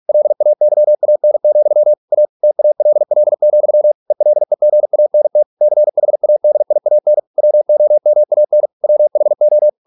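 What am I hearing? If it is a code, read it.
Morse code, heard as a keyed tone, about 46 wpm, 600 Hz. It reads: LAYAN, A TALL, ELEGANT CHADIAN WOMAN WHO
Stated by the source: Morse practice at 45 wpm, the novel Never by Ken Follett in English, from CW med Gnister.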